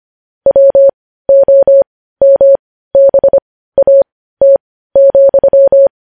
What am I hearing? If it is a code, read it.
Morse code, heard as a keyed tone, about 25 wpm, spaced as for 15 wpm, 550 Hz. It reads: WOMBAT,
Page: news headlines in morse code at 15 wpm